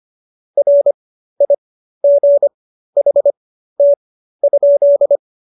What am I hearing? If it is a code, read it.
Morse code, heard as a keyed tone, about 25 wpm, spaced as for 13 wpm, 570 Hz. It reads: RIGHT?